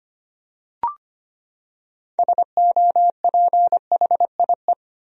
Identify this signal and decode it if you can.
Morse code, heard as a keyed tone, about 25 wpm, 700 Hz. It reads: SOPHIE
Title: USA Common Names - 25wpm